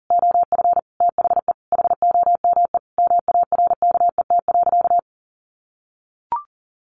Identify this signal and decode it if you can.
Morse code, heard as a keyed tone, about 40 wpm, 700 Hz. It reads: OF THE HOME MARKET.
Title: Sentences from Top 1400 Words 40wpm